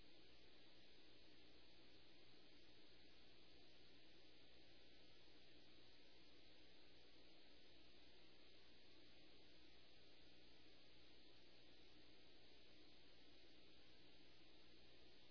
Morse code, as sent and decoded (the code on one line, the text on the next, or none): none